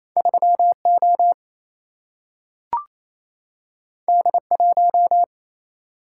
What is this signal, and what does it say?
Morse code, heard as a keyed tone, about 28 wpm, 700 Hz. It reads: 3 O D 1
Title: Single-Letter Single-Number 28wpm